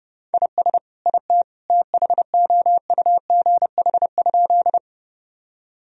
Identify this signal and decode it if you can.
Morse code, heard as a keyed tone, about 30 wpm, 700 Hz. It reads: IS IT THOUGH?
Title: Sentences from Top 1000 Words - Review Part 2 - 30wpm